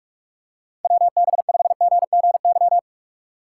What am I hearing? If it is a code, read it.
Morse code, heard as a keyed tone, about 44 wpm, 700 Hz. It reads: WB5GGY